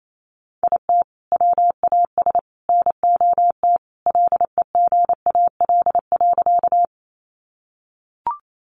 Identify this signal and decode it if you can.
Morse code, heard as a keyed tone, about 28 wpm, 700 Hz. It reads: IT WAS NOT LEGAL.
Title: Sentences from Top 1200 Words 28wpm